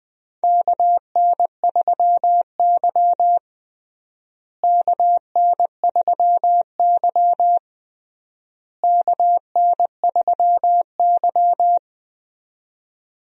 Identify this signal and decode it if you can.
Morse code, heard as a keyed tone, about 20 wpm, 700 Hz. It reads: KN3Y KN3Y KN3Y
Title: USA Call Signs - 4 Characters - Repeated 3x - 20wpm